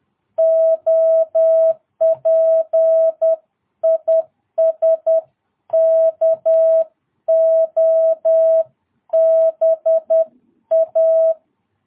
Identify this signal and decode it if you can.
Morse code, heard as a keyed tone, about 10 wpm, 650 Hz. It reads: OP IS KOBA
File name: jh8cbh / 2022cw